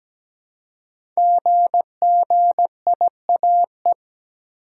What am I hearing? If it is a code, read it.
Morse code, heard as a keyed tone, about 17 wpm, 700 Hz. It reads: GGIAE
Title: Code Groups - 5 Letters - 17wpm